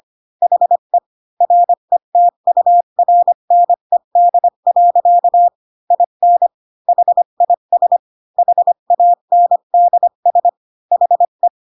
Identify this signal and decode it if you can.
Morse code, heard as a keyed tone, about 25 wpm, 700 Hz. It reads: HE RETURNED. IN HIS HANDS HE
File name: Pinocchio / Pinocchio ch8 20 25 ik1hge